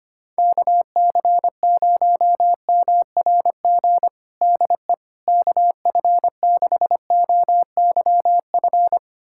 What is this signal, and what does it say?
Morse code, heard as a keyed tone, about 25 wpm, 700 Hz. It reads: KC0MRG DE KF6OYF
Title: USA Callsign DE Callsign - 6 Characters - 25wpm